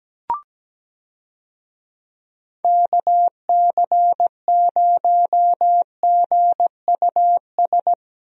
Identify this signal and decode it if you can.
Morse code, heard as a keyed tone, about 17 wpm, 700 Hz. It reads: KC0GUS
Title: USA Call Signs - 6 Characters - Repeated 2x - 17wpm